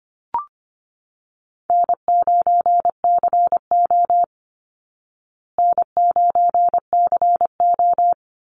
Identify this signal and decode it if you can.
Morse code, heard as a keyed tone, about 25 wpm, 700 Hz. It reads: N9CO N9CO
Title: Easy Call Signs Repeated 2x - 25wpm